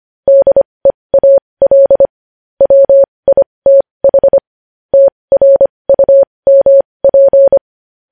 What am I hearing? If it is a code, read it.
Morse code, heard as a keyed tone, about 25 wpm, spaced as for 20 wpm, 550 Hz. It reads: DEAL WITH TRUMP
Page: news-headlines-in-morse-code-at-20-wpm